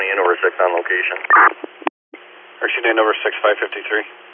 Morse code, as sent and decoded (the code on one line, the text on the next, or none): none